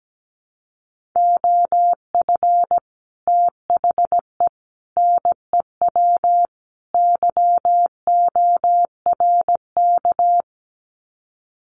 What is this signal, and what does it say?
Morse code, heard as a keyed tone, about 17 wpm, 700 Hz. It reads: OF THE NEW YORK
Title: Sets of 4 Words 17wpm